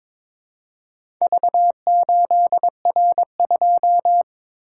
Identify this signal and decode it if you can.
Morse code, heard as a keyed tone, about 22 wpm, 700 Hz. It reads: V8R2
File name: practice / 22wpm / Four Letter-Number Combinations - 22wpm